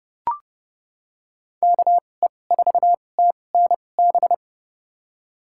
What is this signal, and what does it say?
Morse code, heard as a keyed tone, about 30 wpm, spaced as for 22 wpm, 700 Hz. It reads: KE4TNB
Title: USA Call Signs - 6 Characters - 2x Character Spacing - 30wpm